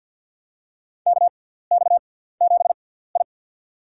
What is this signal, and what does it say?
Morse code, heard as a keyed tone, about 50 wpm, spaced as for 17 wpm, 700 Hz. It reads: KX7I
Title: USA Call Signs - 4 Characters - 6x Character Spacing - 50wpm